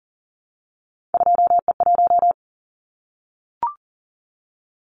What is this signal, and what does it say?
Morse code, heard as a keyed tone, about 38 wpm, 700 Hz. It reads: 2E1
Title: Callsign Prefixes for Top 25 DXCC - 40wpm